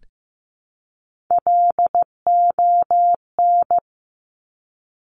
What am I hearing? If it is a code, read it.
Morse code, heard as a keyed tone, about 15 wpm, 700 Hz. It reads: LON
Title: Top Three Letter Combinations in Words 15wpm